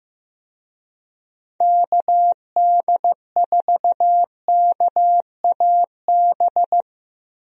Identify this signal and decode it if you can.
Morse code, heard as a keyed tone, about 15 wpm, 700 Hz. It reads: KD4KAB
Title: USA Call Signs - 6 Characters - 15wpm